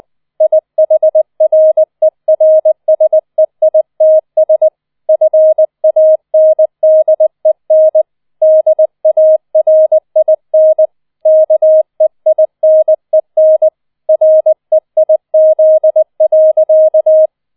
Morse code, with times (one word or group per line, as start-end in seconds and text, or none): 0.4-4.7 IHRERSEITS
5.1-8.0 FANDEN
8.4-10.9 DARIN
11.3-13.7 KEINEN
14.1-17.3 REIZ.